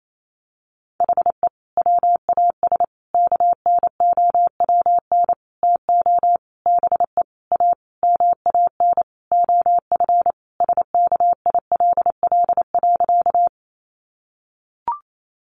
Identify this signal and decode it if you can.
Morse code, heard as a keyed tone, about 28 wpm, 700 Hz. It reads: HE WAS KNOWN TO BE A MAN OF SKILL.
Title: Sentences from Top 1200 Words 28wpm